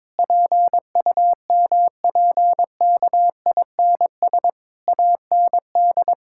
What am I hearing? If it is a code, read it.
Morse code, heard as a keyed tone, about 21 wpm, 700 Hz. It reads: PUMPKINS AND
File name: Difficult Sentences with Hints - 22wpm